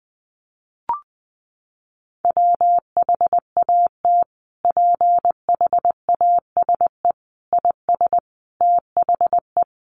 Morse code, heard as a keyed tone, about 20 wpm, 700 Hz. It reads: WHAT PHASE IS THE
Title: Sentences from Top 1200 Words 20wpm